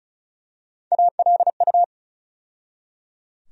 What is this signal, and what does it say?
Morse code, heard as a keyed tone, about 36 wpm, 700 Hz. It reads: ALU
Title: USA Call Sign Postfixes - 3 Letters 35wpm